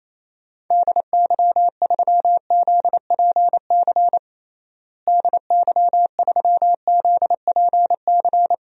Morse code, heard as a keyed tone, about 28 wpm, 700 Hz. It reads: DY3ZPC DY3ZPC
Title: DXCC-100 Call Signs - 6 Characters - Repeated 3x - 28wpm